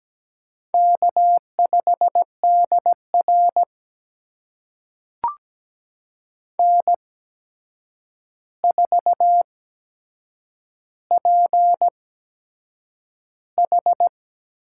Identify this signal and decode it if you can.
Morse code, heard as a keyed tone, about 17 wpm, spaced as for 7 wpm, 700 Hz. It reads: K5DR N4PH
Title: USA Call Signs - 4 Characters - 8x Character Spacing - 17wpm